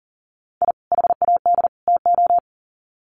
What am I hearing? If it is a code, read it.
Morse code, heard as a keyed tone, about 40 wpm, 700 Hz. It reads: I HAD TO
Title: Sets of 3 Words 40wpm